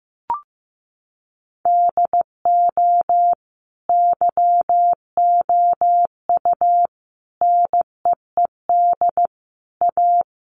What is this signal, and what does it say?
Morse code, heard as a keyed tone, about 15 wpm, 700 Hz. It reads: DO YOU NEED A
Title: Sentences from Top 1500 Words 15wpm